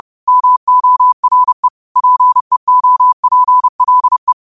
Morse code, heard as a keyed tone, about 30 wpm, 1000 Hz. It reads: MORE PEOPLE
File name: qotdfiles / 30wpm